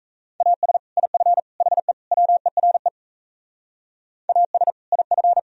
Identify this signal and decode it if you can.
Morse code, heard as a keyed tone, about 32 wpm, 700 Hz. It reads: AS IF HE WERE AS IF
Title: Speed-Racing - Sets of 4 Words 28wpm